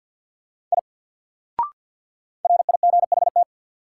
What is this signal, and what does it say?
Morse code, heard as a keyed tone, about 48 wpm, 700 Hz. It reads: I RIGHT
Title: Sentences from Top 200 Words - New Word Drill - 50wpm